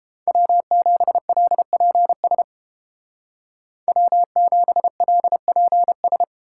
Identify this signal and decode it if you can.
Morse code, heard as a keyed tone, about 31 wpm, 700 Hz. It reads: W7LPS W7LPS